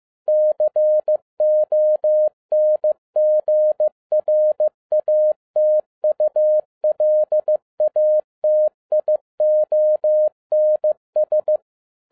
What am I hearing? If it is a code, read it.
Morse code, heard as a keyed tone, about 15 wpm, 610 Hz. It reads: CONGRATULATIONS
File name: 2012-01-mp3 / n17-SUMMARY OF FEA NET IN 2011-15wpm